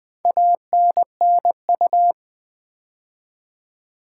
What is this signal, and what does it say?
Morse code, heard as a keyed tone, about 20 wpm, 700 Hz. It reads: ANNU
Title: Build A Word - 2 Letters At A Time - 8-Letter Words - 20wpm